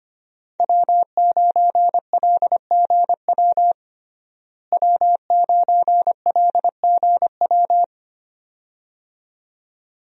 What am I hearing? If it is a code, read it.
Morse code, heard as a keyed tone, about 25 wpm, 700 Hz. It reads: W9LGW W9LGW
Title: USA Call Signs - 5 Characters - Repeated 2x - 25wpm